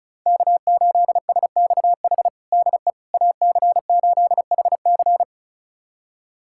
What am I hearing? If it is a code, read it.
Morse code, heard as a keyed tone, about 36 wpm, 700 Hz. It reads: K8SXH DE AC8HC
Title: USA Callsign DE Callsign 35wpm